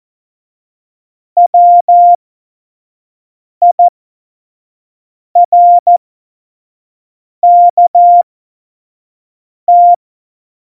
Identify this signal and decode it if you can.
Morse code, heard as a keyed tone, about 14 wpm, spaced as for 5 wpm, 700 Hz. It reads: WIRKT
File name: Morsekurs / Witze / Witz 23. Januar 2018, 25 F70